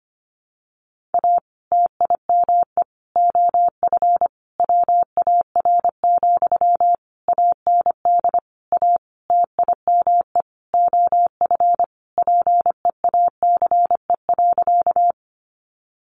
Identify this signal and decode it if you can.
Morse code, heard as a keyed tone, about 25 wpm, 700 Hz. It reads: A TIME OF WAR, AND A TIME OF PEACE.
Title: Sentences from Top 1500 Words 25wpm